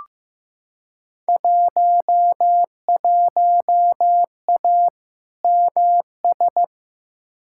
Morse code, heard as a keyed tone, about 15 wpm, 700 Hz. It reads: 11A MS
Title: Field Day Exchanges Encore - 15wpm